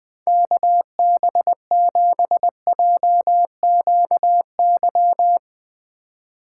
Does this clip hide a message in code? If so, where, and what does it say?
Yes: Morse code from 0.3 to 5.4 s, heard as a keyed tone, about 20 wpm, 700 Hz. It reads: KB7JQY